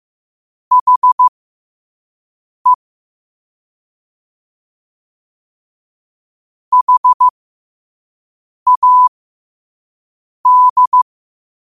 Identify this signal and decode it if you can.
Morse code, heard as a keyed tone, about 15 wpm, spaced as for 5 wpm, 1000 Hz. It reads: HE HAD
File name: qotdfiles / 5wpm